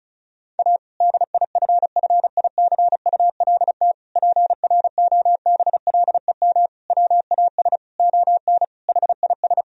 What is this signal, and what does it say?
Morse code, heard as a keyed tone, about 36 wpm, 700 Hz. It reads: A DIFFICULT PROBLEM WAS ON HIS
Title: Sentences from Top 1000 Words - Review Part 2 - 35wpm